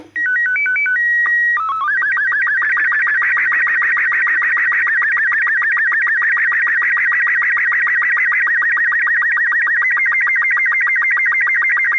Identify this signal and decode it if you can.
SSTV (slow-scan television), Robot 36.
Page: April 12 is the International Day of Human Spaceflight so, what better theme for the April 11-16 Slow Scan TV transmissions.